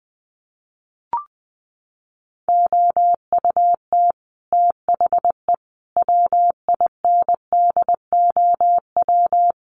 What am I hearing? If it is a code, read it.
Morse code, heard as a keyed tone, about 20 wpm, 700 Hz. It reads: OUT THE WINDOW
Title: Sets of 5 Words Encore2 20wpm